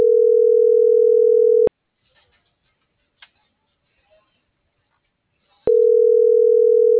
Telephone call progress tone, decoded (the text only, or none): ringback tone